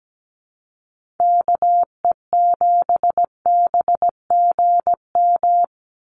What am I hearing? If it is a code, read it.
Morse code, heard as a keyed tone, about 17 wpm, 700 Hz. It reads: KE7BGM
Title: NA SOTA Activator Callsigns 17wpm